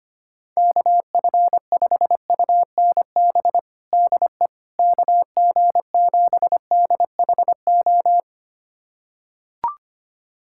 Morse code, heard as a keyed tone, about 25 wpm, 700 Hz. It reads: KF5UNB DE KG7DHO
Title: USA Callsign DE Callsign - 6 Characters - 25wpm